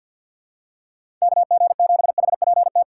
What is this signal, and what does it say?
Morse code, heard as a keyed tone, about 52 wpm, 700 Hz. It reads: KG7HPT
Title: USA Call Signs - 6 Characters - 4x Character Spacing - 50wpm